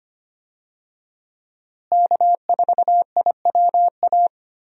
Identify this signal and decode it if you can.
Morse code, heard as a keyed tone, about 25 wpm, 700 Hz. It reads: K4IWA
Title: USA Call Signs - 5 Characters - Repeated 3x - 25wpm